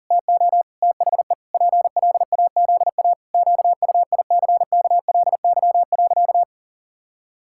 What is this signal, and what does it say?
Morse code, heard as a keyed tone, about 40 wpm, 700 Hz. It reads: TO THE PLAZA QUICKLY.
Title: Pangram Challenge v2 - 40wpm